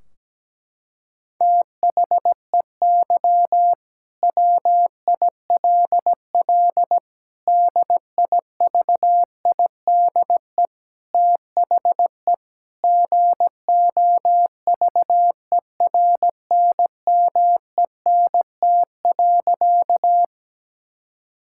Morse code, heard as a keyed tone, about 17 wpm, 700 Hz. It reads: THEY WILL DIVIDE THE GOVERNMENT.
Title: Sentences from Top 1600 Words 17wpm